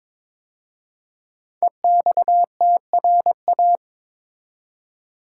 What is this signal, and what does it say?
Morse code, heard as a keyed tone, about 22 wpm, 700 Hz. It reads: EXTRA
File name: 5 Letter Words - First 3 Letters Given - 22wpm